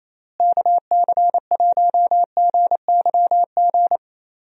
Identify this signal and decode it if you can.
Morse code, heard as a keyed tone, about 27 wpm, 700 Hz. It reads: KC1GYG
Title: USA Call Signs - 6 Characters - Repeated 2x - 28wpm